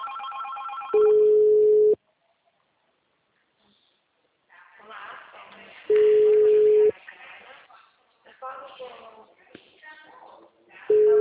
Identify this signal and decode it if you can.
Ringback tone (Europe, 425 Hz), with speech.
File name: out- -2041- - - .22610